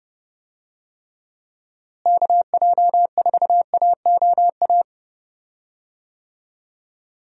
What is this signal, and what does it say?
Morse code, heard as a keyed tone, about 29 wpm, 700 Hz. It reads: KJ4AOA
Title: USA Call Signs - 6 Characters - Repeated 2x - 30wpm